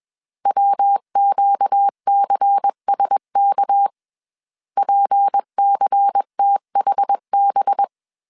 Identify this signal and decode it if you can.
Morse code, heard as a keyed tone, about 21 wpm, 800 Hz. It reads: WQCSK PCTHB
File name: CW / DET / 1 6 1500C S B 20wpm 2048